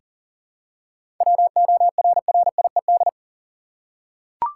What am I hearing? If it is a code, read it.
Morse code, heard as a keyed tone, about 40 wpm, 700 Hz. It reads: WORRIED
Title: Seven Letter Words - 40wpm